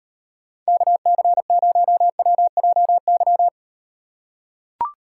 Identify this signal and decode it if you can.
Morse code, heard as a keyed tone, about 38 wpm, 700 Hz. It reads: KC0WJY